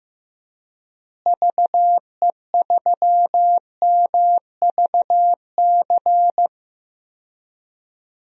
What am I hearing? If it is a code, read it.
Morse code, heard as a keyed tone, about 15 wpm, 700 Hz. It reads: VE3MVC
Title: DXCC-100 Call Signs - 6 Characters - Repeated 3x - 15wpm